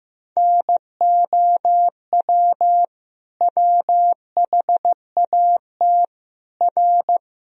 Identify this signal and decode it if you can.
Morse code, heard as a keyed tone, about 15 wpm, 700 Hz. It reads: NOW WHAT R
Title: Sentences from Top 100 Words and Abbreviations 15wpm